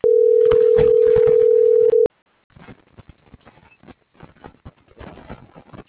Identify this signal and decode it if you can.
Ringback tone (North America), 440 + 480 Hz.